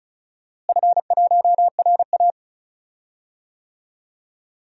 Morse code, heard as a keyed tone, about 36 wpm, 700 Hz. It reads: F1RA
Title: DXCC-100 Call Signs - 4 Characters - Repeated 3x - 35wpm